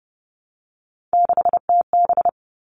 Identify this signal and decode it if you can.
Morse code, heard as a keyed tone, about 30 wpm, 700 Hz. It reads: 6TB
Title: All Cut Numbers - 3-Digit 30wpm